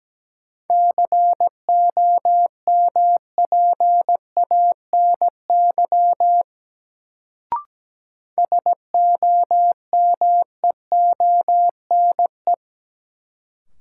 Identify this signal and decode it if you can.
Morse code, heard as a keyed tone, about 17 wpm, 700 Hz. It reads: COMPANY SOMEONE
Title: Sentences from Top 600 Words - New Word Drill - 17wpm